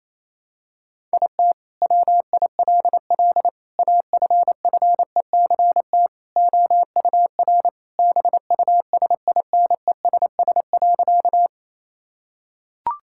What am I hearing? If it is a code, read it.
Morse code, heard as a keyed tone, about 28 wpm, 700 Hz. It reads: IT WILL AFFECT OUR BUSINESS.
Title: Sentences from Top 1000 Words - Review Part 6 - 28wpm